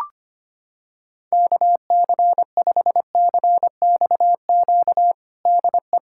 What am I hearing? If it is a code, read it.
Morse code, heard as a keyed tone, about 25 wpm, 700 Hz. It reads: KC5CXQ DE
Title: USA Callsign DE Callsign 25wpm